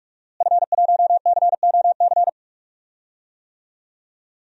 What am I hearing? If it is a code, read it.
Morse code, heard as a keyed tone, about 46 wpm, 700 Hz. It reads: F1COC
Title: DXCC-100 Call Signs - 5 Characters - Repeated 2x - 45wpm